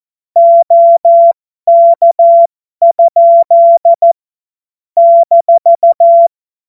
Morse code, heard as a keyed tone, about 14 wpm, spaced as for 12 wpm, 680 Hz. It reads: OK? -
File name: Morsekurs / Witze